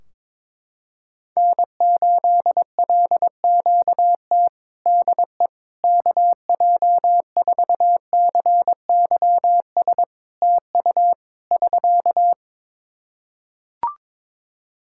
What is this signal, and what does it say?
Morse code, heard as a keyed tone, about 22 wpm, 700 Hz. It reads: N8LQT DE KJ4CYS TU <SK>
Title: K1USN SST QSO Practice 22wpm